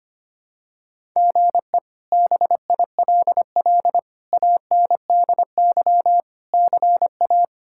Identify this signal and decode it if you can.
Morse code, heard as a keyed tone, about 25 wpm, 700 Hz. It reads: GE BILL ANDY CA
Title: K1USN SST QSO Practice - 25wpm